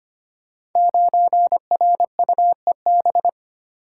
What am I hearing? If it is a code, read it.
Morse code, heard as a keyed tone, about 25 wpm, 700 Hz. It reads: 9RUEB
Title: Code Groups - 5 Letter:Numbers - 25wpm